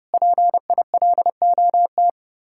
Morse code, heard as a keyed tone, about 37 wpm, 700 Hz. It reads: PILOT
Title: Speed-Racing Common QSO Phrases - 30wpm